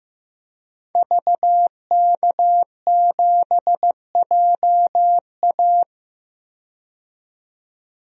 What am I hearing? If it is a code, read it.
Morse code, heard as a keyed tone, about 15 wpm, 700 Hz. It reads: VK7JA
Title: DXCC-100 Call Signs - 5 Characters - 15wpm